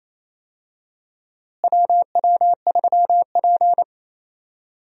Morse code, heard as a keyed tone, about 28 wpm, 700 Hz. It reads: WW3P
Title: USA Call Signs - 4 Characters - Repeated 3x - 28wpm